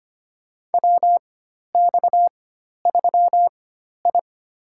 Morse code, heard as a keyed tone, about 25 wpm, spaced as for 12 wpm, 700 Hz. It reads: WX3I
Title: USA Call Signs - 4x Character Spacing - 25wpm